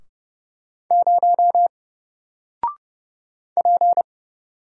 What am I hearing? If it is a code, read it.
Morse code, heard as a keyed tone, about 29 wpm, 700 Hz. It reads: 0 P
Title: Single Letter-Number Alt 30wpm